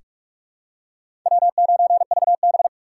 Morse code, heard as a keyed tone, about 46 wpm, 700 Hz. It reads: W9VB
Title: USA Call Signs - 4 Characters - 2x Character Spacing - 45wpm